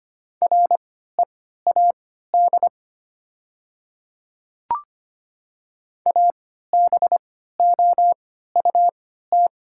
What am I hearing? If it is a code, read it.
Morse code, heard as a keyed tone, about 25 wpm, spaced as for 14 wpm, 700 Hz. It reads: READ ABOUT
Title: 3x Character Spacing - Top 200 Words - 25wpm